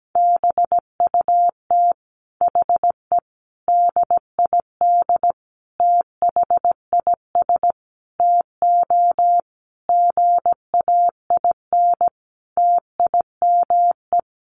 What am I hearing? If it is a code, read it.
Morse code, heard as a keyed tone, about 17 wpm, 700 Hz. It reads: BUT HE DID THIS TO GAIN TIME
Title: Sentences from Top 1400 Words 17wpm